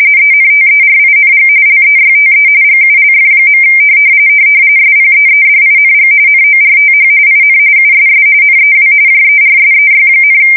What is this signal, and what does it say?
RTTY, about 45.45 baud. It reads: ENTERING LEYTE GULF. SPEEDING NORTH IN USS NEW JERSEY,
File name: fsk170 / battleshipnj